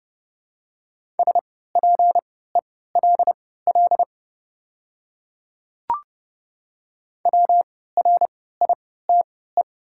Morse code, heard as a keyed tone, about 29 wpm, spaced as for 17 wpm, 700 Hz. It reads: SPELL WRITE
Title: 3x Character Spacing - Top 300 Words - 30wpm